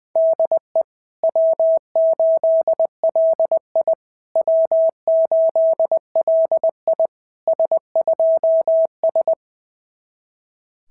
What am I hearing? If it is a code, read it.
Morse code, heard as a keyed tone, about 20 wpm, 660 Hz. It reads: DE W8LI W8LI S2S
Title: SOTA QSO Practice 20wpm